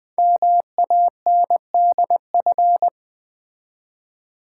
Morse code, heard as a keyed tone, about 20 wpm, 700 Hz. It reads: MANDF
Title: Code Groups - 5 Letters - 20wpm